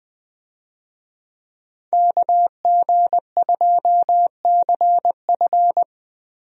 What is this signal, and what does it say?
Morse code, heard as a keyed tone, about 21 wpm, 700 Hz. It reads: KG2CF
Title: USA Call Signs - 5 Characters - Repeated 3x - 20wpm